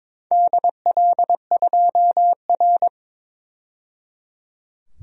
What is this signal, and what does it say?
Morse code, heard as a keyed tone, about 22 wpm, 700 Hz. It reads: DL2R
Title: DXCC-100 Call Signs - 4 Characters - 22wpm